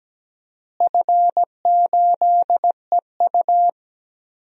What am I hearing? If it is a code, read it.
Morse code, heard as a keyed tone, about 17 wpm, 700 Hz. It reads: F8EU
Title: DXCC-100 Call Signs - 4 Characters - Repeated 2x - 17wpm